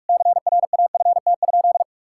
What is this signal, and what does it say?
Morse code, heard as a keyed tone, about 46 wpm, 700 Hz. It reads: CRAFT?
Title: Sentences from Top 1000 Words - Review Part 6 - 45wpm